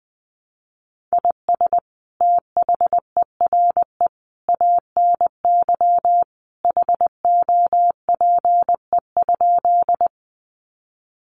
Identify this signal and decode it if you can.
Morse code, heard as a keyed tone, about 20 wpm, 700 Hz. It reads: IS THERE ANY HOPE?
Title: Sentences from Top 1000 Words 20wpm